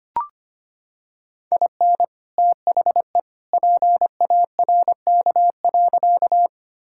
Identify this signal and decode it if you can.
Morse code, heard as a keyed tone, about 25 wpm, 700 Hz. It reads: IN THE PARK.
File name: practice / 25wpm / Sentences with 2-Word Phrases - 25wpm